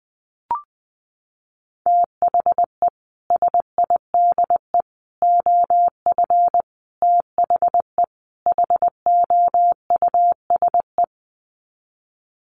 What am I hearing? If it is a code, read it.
Morse code, heard as a keyed tone, about 19 wpm, 700 Hz. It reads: THE SIDE OF THE HOUSE
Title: Sets of 5 Words Encore2 20wpm